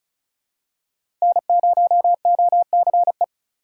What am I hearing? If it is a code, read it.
Morse code, heard as a keyed tone, about 36 wpm, 700 Hz. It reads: N0OCE
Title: USA Call Signs - 5 Characters - 2x Character Spacing - 35wpm